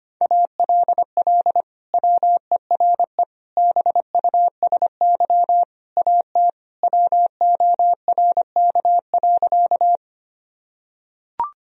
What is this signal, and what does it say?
Morse code, heard as a keyed tone, about 25 wpm, 700 Hz. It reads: ALL WERE BUSY AT WORK.
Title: Sentences from Top 1000 Words - Review Part 3 - 25wpm